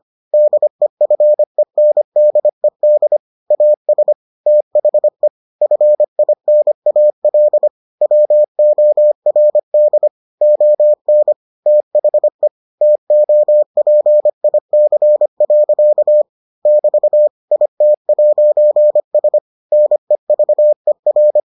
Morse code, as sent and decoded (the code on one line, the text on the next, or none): -.. ..-. -. -.. -.. / .- ... / - .... / ..-. .. -. .- .-.. / .-- --- .-. -.. / --- -. / - .... / - --- .--. .. -.-. .-.-.- / -...- / .. - .----. ... / -. ...- . .-.
DEFENDED AS THE FINAL WORD ON THE TOPIC. <BT> IT'S NEVER